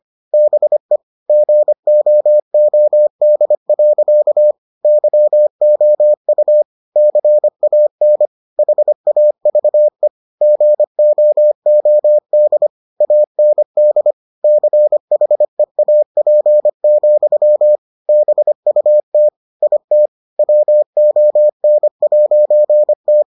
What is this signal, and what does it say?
Morse code, heard as a keyed tone, about 25 wpm, 600 Hz. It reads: BE GOOD. YOU CAN HAVE GOOD AND CHEAP, BUT IT WON'T